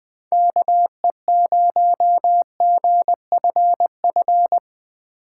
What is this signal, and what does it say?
Morse code, heard as a keyed tone, about 20 wpm, 700 Hz. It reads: KE0GFF
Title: USA Call Signs - 2x Character Spacing - 20wpm